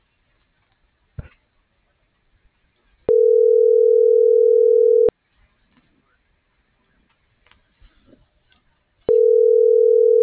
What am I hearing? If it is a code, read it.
Ringback tone (North America), 440 + 480 Hz.